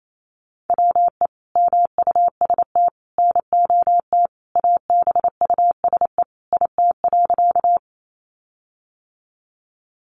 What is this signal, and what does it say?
Morse code, heard as a keyed tone, about 28 wpm, 700 Hz. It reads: WE MUST NOT ABUSE IT.
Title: Sentences from Top 1300 Words 28wpm